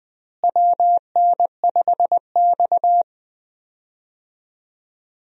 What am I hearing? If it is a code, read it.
Morse code, heard as a keyed tone, about 20 wpm, 700 Hz. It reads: WN5X
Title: USA Call Signs - 4 Characters - Repeated 3x - 20wpm